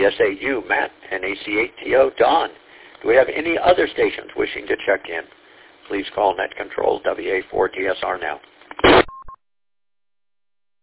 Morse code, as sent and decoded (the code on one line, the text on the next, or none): none